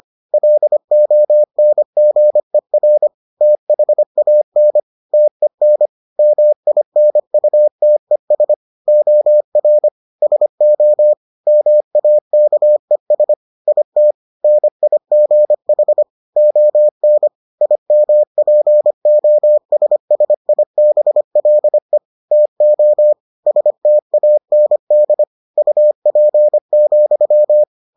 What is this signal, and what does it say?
Morse code, heard as a keyed tone, about 25 wpm, 600 Hz. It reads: LONGER THAN TEN MINUTES OR SO MAKES IT NIGH ON IMPOSSIBLE TO STAND UP,